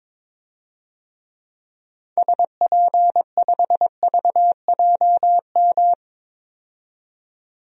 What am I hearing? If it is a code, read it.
Morse code, heard as a keyed tone, about 22 wpm, 700 Hz. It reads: SP5VJM
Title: DXCC-100 Call Signs - 6 Characters - 22wpm